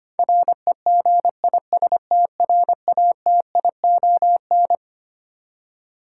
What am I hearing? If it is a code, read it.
Morse code, heard as a keyed tone, about 25 wpm, 700 Hz. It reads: REGISTRATION
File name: practice / 25wpm / Twelve Letter Words - 25wpm